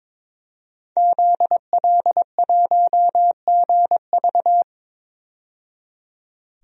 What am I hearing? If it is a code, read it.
Morse code, heard as a keyed tone, about 22 wpm, 700 Hz. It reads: ZL1GV
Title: DXCC-100 Call Signs - 5 Characters - 22wpm